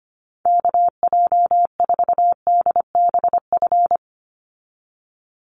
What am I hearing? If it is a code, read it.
Morse code, heard as a keyed tone, about 25 wpm, 700 Hz. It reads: KJ4DBF
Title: USA Call Signs 25wpm